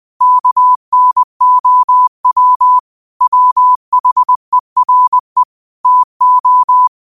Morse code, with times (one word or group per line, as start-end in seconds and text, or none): 0.2-2.8 KNOW
3.2-5.4 WHERE
5.9-6.9 TO